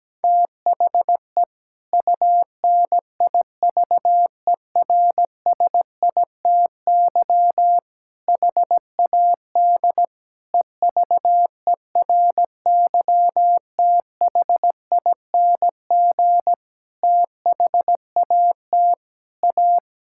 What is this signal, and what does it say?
Morse code, heard as a keyed tone, about 17 wpm, 700 Hz. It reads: THE UNIVERSITY HAD EVERYTHING THAT A
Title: Sentences from Top 1000 Words - Review Part 5 - 17wpm